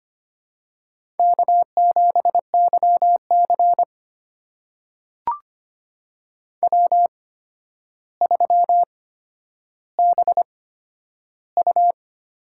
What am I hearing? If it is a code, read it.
Morse code, heard as a keyed tone, about 25 wpm, spaced as for 11 wpm, 700 Hz. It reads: K7YC W3BU
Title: USA Call Signs - 4 Characters - 8x Character Spacing - 25wpm